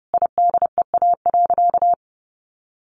Morse code, heard as a keyed tone, about 30 wpm, 700 Hz. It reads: IDEA.
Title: Sentences from Top 1000 Words 30wpm